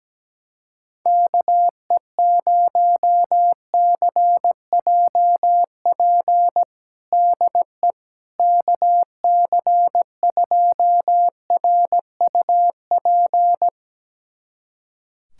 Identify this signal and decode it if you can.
Morse code, heard as a keyed tone, about 17 wpm, 700 Hz. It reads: KE0CJP DE KC2RUP
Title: USA Callsign DE Callsign - 6 Characters - 17wpm